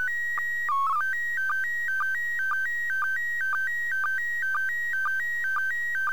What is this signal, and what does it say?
SSTV (slow-scan television), PD 120.
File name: 2020.10.02 ISS 2